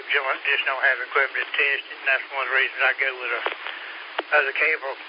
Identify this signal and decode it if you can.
Emergency Alert System (SAME) tones, heard as harsh buzzing bursts, with speech. It is heard from 0.7 to 4.8 s.